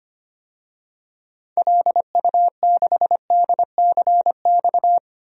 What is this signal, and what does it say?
Morse code, heard as a keyed tone, about 25 wpm, 700 Hz. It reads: LU6DCX
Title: DXCC-100 Call Signs - 6 Characters - 25wpm